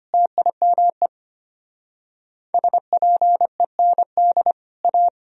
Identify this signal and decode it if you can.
Morse code, heard as a keyed tone, about 32 wpm, 700 Hz. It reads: TIME SPEND A